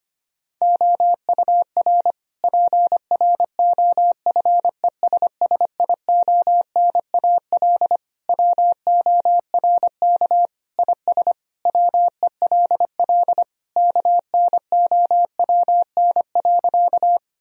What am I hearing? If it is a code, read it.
Morse code, heard as a keyed tone, about 25 wpm, 700 Hz. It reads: OUR PROFESSIONAL WORK IS WELL KNOWN.
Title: Sentences from Top 1000 Words - Review Part 4 - 25wpm